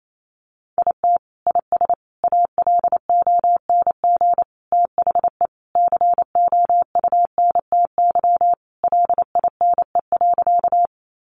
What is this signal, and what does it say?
Morse code, heard as a keyed tone, about 28 wpm, 700 Hz. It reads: IT IS ALONG THE COUNTY LINE.
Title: Sentences from Top 1200 Words 28wpm